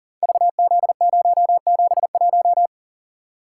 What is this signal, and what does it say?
Morse code, heard as a keyed tone, about 39 wpm, 700 Hz. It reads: VZ071
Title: Code Groups - 5 Letter:Numbers - 40wpm